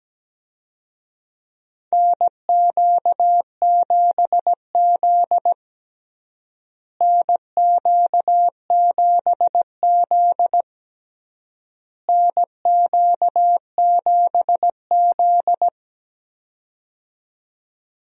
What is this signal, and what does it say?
Morse code, heard as a keyed tone, about 17 wpm, 700 Hz. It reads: NQ7Z NQ7Z NQ7Z
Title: USA Call Signs - 4 Characters - Repeated 3x - 17wpm